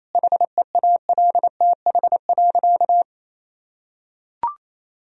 Morse code, heard as a keyed tone, about 28 wpm, 700 Hz. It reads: HEALTH.
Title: Sentences from Top 1000 Words - Review Part 3 - 28wpm